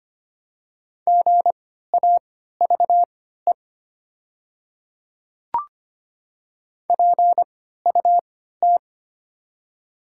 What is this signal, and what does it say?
Morse code, heard as a keyed tone, about 25 wpm, spaced as for 14 wpm, 700 Hz. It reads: GAVE PUT
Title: 3x Character Spacing - Top 500 Words - 25wpm